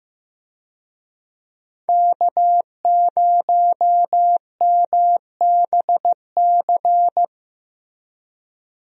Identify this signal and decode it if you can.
Morse code, heard as a keyed tone, about 15 wpm, 700 Hz. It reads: K0MBC
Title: USA Call Signs - 5 Characters - Repeated 3x - 15wpm